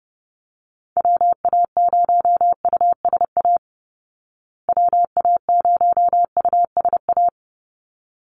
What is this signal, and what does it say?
Morse code, heard as a keyed tone, about 30 wpm, 700 Hz. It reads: WA0USA WA0USA
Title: Call Signs Repeated 3x - 30wpm